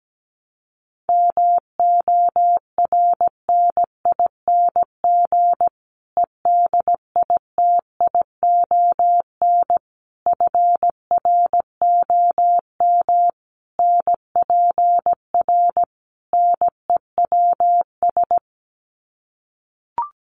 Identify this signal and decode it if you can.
Morse code, heard as a keyed tone, about 17 wpm, 700 Hz. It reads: MORNING EDITION FROM NPR NEWS
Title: Sets of 5 Words Encore 17wpm